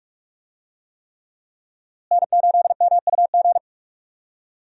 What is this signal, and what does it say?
Morse code, heard as a keyed tone, about 45 wpm, 700 Hz. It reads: N8MUG
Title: USA Call Signs - 5 Characters - Repeated 3x - 45wpm